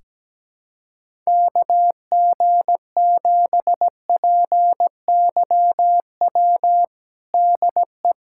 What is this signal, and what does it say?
Morse code, heard as a keyed tone, about 17 wpm, 700 Hz. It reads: KG7PYW DE
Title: USA Callsign DE Callsign - 6 Characters - 17wpm